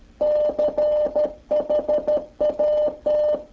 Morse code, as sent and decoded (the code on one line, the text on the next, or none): -.-. .... .- -
CHAT